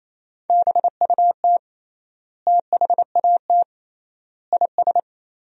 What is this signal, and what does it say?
Morse code, heard as a keyed tone, about 28 wpm, 700 Hz. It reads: BUT THAT IS
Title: 3x Word-Spacing - Sentences from Top 500 Words - 28wpm